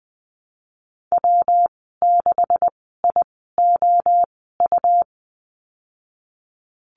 Morse code, heard as a keyed tone, about 20 wpm, spaced as for 15 wpm, 700 Hz. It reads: W6IOU
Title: 193 Lesson 31g - Letter v - Call Signs 20wpm